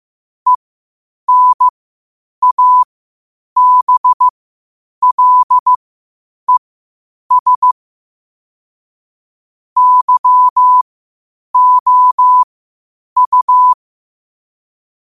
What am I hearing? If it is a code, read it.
Morse code, heard as a keyed tone, about 15 wpm, spaced as for 8 wpm, 1000 Hz. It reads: ENABLES YOU
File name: qotdfiles / 8wpm